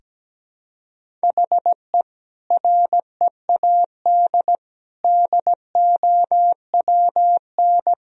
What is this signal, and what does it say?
Morse code, heard as a keyed tone, about 17 wpm, 700 Hz. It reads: HE READ DOWN
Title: Sentences from Top 1000 Words - Review Part 1 - 17wpm